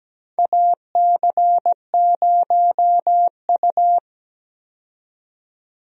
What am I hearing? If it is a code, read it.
Morse code, heard as a keyed tone, about 17 wpm, 700 Hz. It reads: AC0U
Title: USA Call Signs - 4 Characters - 17wpm